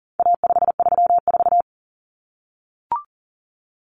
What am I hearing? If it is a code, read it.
Morse code, heard as a keyed tone, about 40 wpm, 700 Hz. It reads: A534
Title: Common Cut Numbers - 4-Digit 40wpm